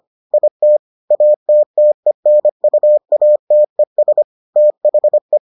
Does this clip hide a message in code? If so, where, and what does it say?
Yes: Morse code from 0.3 to 5.4 s, heard as a keyed tone, about 25 wpm, 600 Hz. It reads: IT ATTENUATES THE